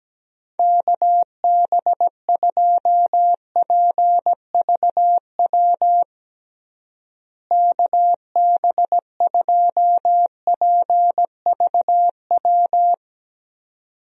Morse code, heard as a keyed tone, about 17 wpm, 700 Hz. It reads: KB2PVW KB2PVW
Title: USA Call Signs - 6 Characters - Repeated 2x - 17wpm